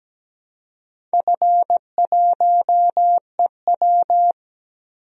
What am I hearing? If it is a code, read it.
Morse code, heard as a keyed tone, about 17 wpm, 700 Hz. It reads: F1EW
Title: DXCC-100 Call Signs - 4 Characters - 17wpm